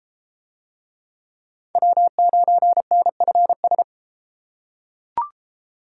Morse code, heard as a keyed tone, about 31 wpm, 700 Hz. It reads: W9NFS